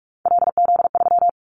Morse code, heard as a keyed tone, about 46 wpm, 700 Hz. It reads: LZ3
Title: Callsign Prefixes for Top 50 DXCC - 45wpm